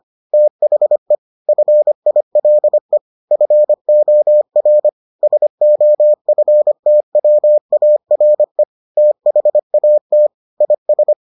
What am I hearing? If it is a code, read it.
Morse code, heard as a keyed tone, about 25 wpm, 600 Hz. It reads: THE FILE FOR SOFTWARE THAT IS